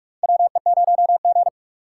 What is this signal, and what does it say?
Morse code, heard as a keyed tone, about 46 wpm, 700 Hz. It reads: WE0G
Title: USA Call Signs - 4 Characters - 3x Character Spacing - 45wpm